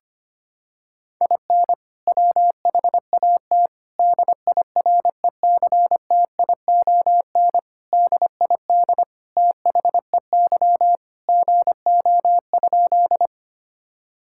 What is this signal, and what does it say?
Morse code, heard as a keyed tone, about 25 wpm, 700 Hz. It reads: IN WHAT DIRECTION DID THEY GO?
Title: Sentences from Top 1000 Words - Review Part 5 - 25wpm